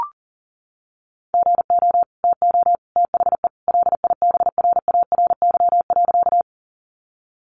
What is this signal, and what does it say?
Morse code, heard as a keyed tone, about 40 wpm, 700 Hz. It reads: GO TO THE LIBRARY.
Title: Sentences from Top 600 Words 40wpm